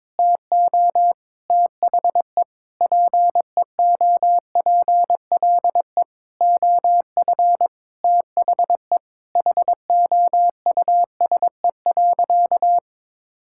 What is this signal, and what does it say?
Morse code, heard as a keyed tone, about 22 wpm, 700 Hz. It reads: TO THE PEOPLE OF THE HOUSE.
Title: Sentences from Top 1000 Words - Review Part 5 - 22wpm